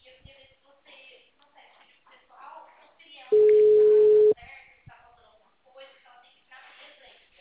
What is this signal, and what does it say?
Ringback tone (Europe, 425 Hz), with speech.